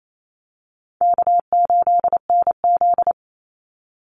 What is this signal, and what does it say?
Morse code, heard as a keyed tone, about 28 wpm, 700 Hz. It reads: K8NZ
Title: Call Signs - 28wpm